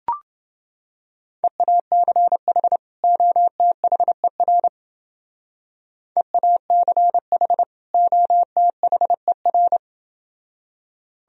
Morse code, heard as a keyed tone, about 28 wpm, 700 Hz. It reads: EACH OTHER EACH OTHER